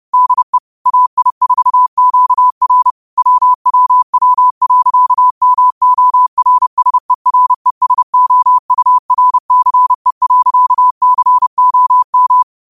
Morse code, heard as a keyed tone, about 30 wpm, 1000 Hz. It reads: DE AI4QR WWW.MORSERESOURCE.COM
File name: qotdfiles / 30wpm